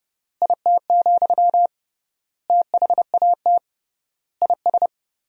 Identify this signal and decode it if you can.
Morse code, heard as a keyed tone, about 29 wpm, 700 Hz. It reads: IT, THAT IS
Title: 3x Word-Spacing - Sentences from Top 400 Words - 30wpm